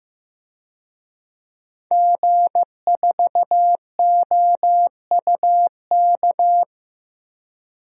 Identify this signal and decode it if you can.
Morse code, heard as a keyed tone, about 15 wpm, 700 Hz. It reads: G4OUK
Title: DXCC-100 Call Signs - 5 Characters - Repeated 3x - 15wpm